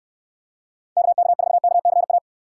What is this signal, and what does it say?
Morse code, heard as a keyed tone, about 69 wpm, 700 Hz. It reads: KK4KCA